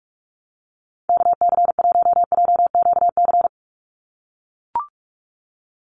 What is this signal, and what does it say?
Morse code, heard as a keyed tone, about 44 wpm, 700 Hz. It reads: KC1JQC
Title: USA Call Signs 45wpm